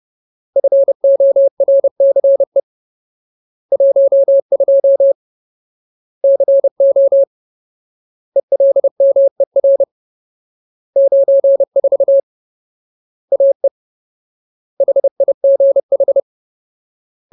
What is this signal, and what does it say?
Morse code, heard as a keyed tone, about 30 wpm, 550 Hz. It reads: FORCE 12 CO ELMER 94 AE HIGH
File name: Ultimate Word List / 30WPM Ultimate